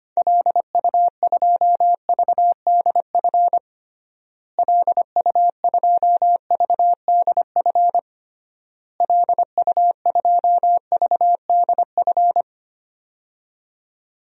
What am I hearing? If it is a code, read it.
Morse code, heard as a keyed tone, about 25 wpm, 700 Hz. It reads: LU2VDF LU2VDF LU2VDF
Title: DXCC-100 Call Signs - 6 Characters - Repeated 3x - 25wpm